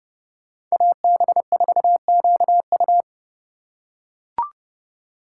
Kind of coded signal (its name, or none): Morse code